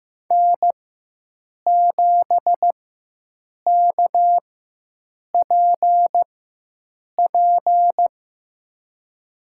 Morse code, heard as a keyed tone, about 16 wpm, spaced as for 7 wpm, 700 Hz. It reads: N7KPP
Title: USA Call Signs - 5 Characters - 4x Character Spacing - 15wpm